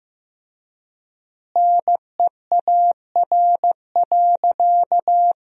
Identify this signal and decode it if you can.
Morse code, heard as a keyed tone, about 15 wpm, 700 Hz. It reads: NEAR.
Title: 4x Word-Spacing - Sentences from Top 300 Words - 15wpm